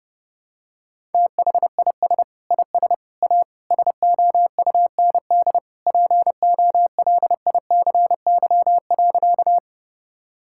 Morse code, heard as a keyed tone, about 32 wpm, 700 Hz. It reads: THIS IS A SOUND POLICY.
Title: Sentences from Top 1000 Words - Review Part 5 - 30wpm